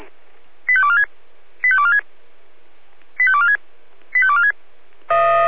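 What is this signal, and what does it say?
Five-tone selective call (ZVEI), heard as quick runs of five tones, tones of about 70 ms. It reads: 85246 85246 85246 85246
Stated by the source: ZVEI five-tone selective call 85310 received by SDR.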